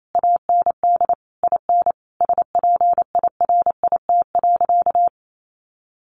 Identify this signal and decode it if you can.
Morse code, heard as a keyed tone, about 28 wpm, 700 Hz. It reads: AND IN SPIRIT.
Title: Sentences from Top 1400 Words 28wpm